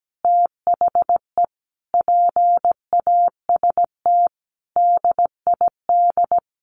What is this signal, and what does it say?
Morse code, heard as a keyed tone, about 17 wpm, 700 Hz. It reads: THE PAST DID
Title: Sentences from Top 1400 Words 17wpm